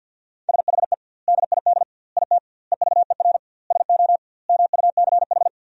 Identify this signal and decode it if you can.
Morse code, heard as a keyed tone, about 50 wpm, 700 Hz. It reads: SHE DID IT EVER SO MUCH
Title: 2x Word-Spacing - Sentences from Top 400 Words - 50wpm